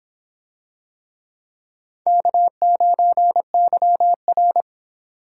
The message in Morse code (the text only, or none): K9YR